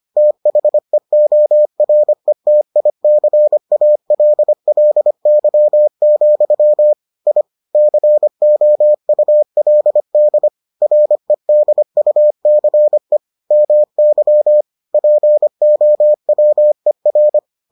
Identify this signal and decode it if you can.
Morse code, heard as a keyed tone, about 25 wpm, 600 Hz. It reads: THEORETICALLY, I COULD REDUCE MY POWER